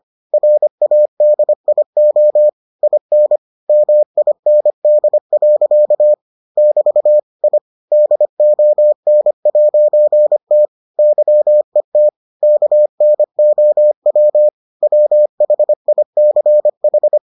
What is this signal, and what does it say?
Morse code, heard as a keyed tone, about 25 wpm, 600 Hz. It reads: RADIO IN MIND. <BT> I DON'T YET KNOW WHICH